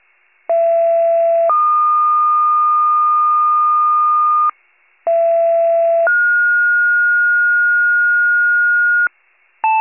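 Two-tone page, heard as a one-second tone, then a three-second tone, about 670 then 1220 Hz, about 670 then 1430 Hz, with noise. It is heard from 0.5 to 9.1 s.
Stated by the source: County-wide paging alert, WCFM pager archive.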